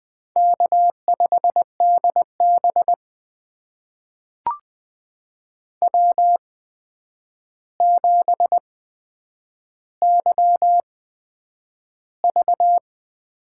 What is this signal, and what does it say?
Morse code, heard as a keyed tone, about 20 wpm, spaced as for 9 wpm, 700 Hz. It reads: K5DB W7YV
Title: USA Call Signs - 8x Character Spacing - 20wpm